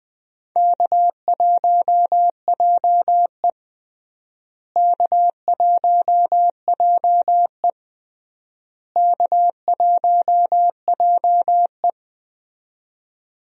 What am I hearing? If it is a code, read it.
Morse code, heard as a keyed tone, about 20 wpm, 700 Hz. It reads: K1JE K1JE K1JE
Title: USA Call Signs - 4 Characters - Repeated 3x - 20wpm